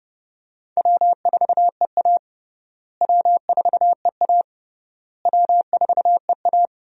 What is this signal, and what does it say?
Morse code, heard as a keyed tone, about 31 wpm, 700 Hz. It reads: W4EA W4EA W4EA
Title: USA Call Signs - 4 Characters - Repeated 3x - 30wpm